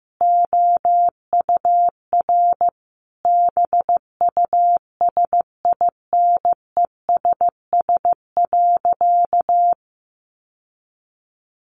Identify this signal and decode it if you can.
Morse code, heard as a keyed tone, about 15 wpm, 700 Hz. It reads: OUR BUSINESS.
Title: Sentences from Top 1400 Words 15wpm